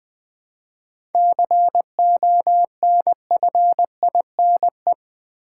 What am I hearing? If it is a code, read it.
Morse code, heard as a keyed tone, about 20 wpm, 700 Hz. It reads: CONFINE
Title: 7 Letter Words - First 3 Letters Given - 20wpm